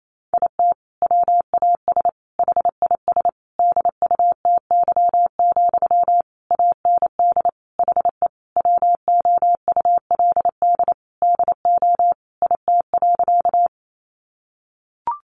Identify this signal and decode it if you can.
Morse code, heard as a keyed tone, about 28 wpm, 700 Hz. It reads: IT WAS HIS DUTY, AND HE WOULD DO IT.
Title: Sentences from Top 1500 Words 28wpm